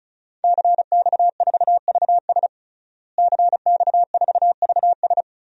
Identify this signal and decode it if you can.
Morse code, heard as a keyed tone, about 36 wpm, 700 Hz. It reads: CX4VS CX4VS
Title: DXCC-100 Call Signs - 5 Characters - Repeated 3x - 35wpm